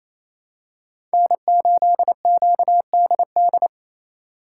Morse code, heard as a keyed tone, about 28 wpm, 700 Hz. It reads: N8QDD
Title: USA Call Signs - 3x Character Spacing - 28wpm